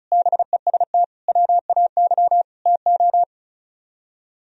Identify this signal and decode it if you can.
Morse code, heard as a keyed tone, about 43 wpm, 700 Hz. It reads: BEST WAY TO
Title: Speed-Racing - Sets of 4 Words 35wpm